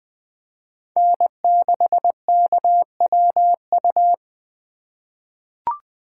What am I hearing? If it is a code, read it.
Morse code, heard as a keyed tone, about 20 wpm, 700 Hz. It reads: N6KWU
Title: USA Call Signs - 5 Characters - 2x Character Spacing - 20wpm